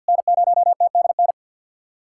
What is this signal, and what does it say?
Morse code, heard as a keyed tone, about 50 wpm, 690 Hz. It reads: N0TDN
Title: SOTA QSO Practice 50wpm